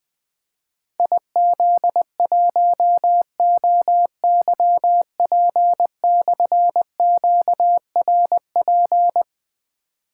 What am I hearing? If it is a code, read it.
Morse code, heard as a keyed tone, about 20 wpm, 700 Hz. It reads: IZ1OYP/QRP
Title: Difficult Call Signs 20wpm